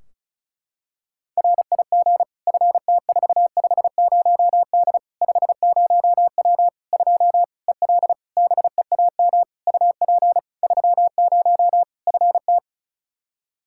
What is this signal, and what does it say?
Morse code, heard as a keyed tone, about 36 wpm, 700 Hz. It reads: RIG FT450D 50W 2 EL BEAM UP 30 FT
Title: Practice QSO - Exchange 2 - Info - 35wpm